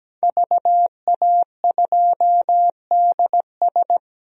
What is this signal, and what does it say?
Morse code, heard as a keyed tone, about 17 wpm, 700 Hz. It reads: VA2DS
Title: DXCC-100 Call Signs - 5 Characters - 17wpm